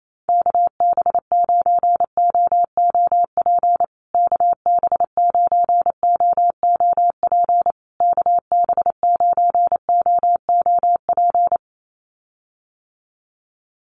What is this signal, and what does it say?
Morse code, heard as a keyed tone, about 28 wpm, 700 Hz. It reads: KB9OOP KB9OOP KB9OOP
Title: USA Call Signs Repeated 3x 28wpm